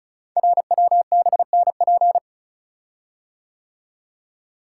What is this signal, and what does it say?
Morse code, heard as a keyed tone, about 36 wpm, 700 Hz. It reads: RWBNP
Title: Code Groups - 5 Letters - 35wpm